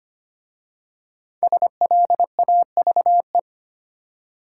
Morse code, heard as a keyed tone, about 25 wpm, 700 Hz. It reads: SLAVE
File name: 5 Letter Words - First 3 Letters Given - 25wpm